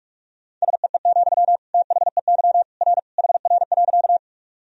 Morse code, heard as a keyed tone, about 46 wpm, 700 Hz. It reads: SEE, THEY R HR.
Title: Sentences from Top 200 Words and Abbreviations 45wpm